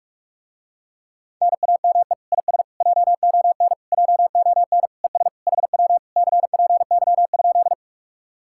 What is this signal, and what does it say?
Morse code, heard as a keyed tone, about 44 wpm, 700 Hz. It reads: NAME IS JON JON ES HW CPY?